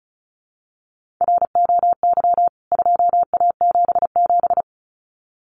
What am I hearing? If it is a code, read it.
Morse code, heard as a keyed tone, about 34 wpm, 700 Hz. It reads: ROY 2A77
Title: CWT Exchange 35wpm